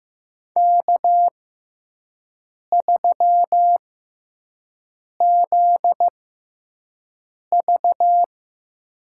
Morse code, heard as a keyed tone, about 15 wpm, spaced as for 5 wpm, 700 Hz. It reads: K3ZV
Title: USA Call Signs - 6x Character Spacing - 15wpm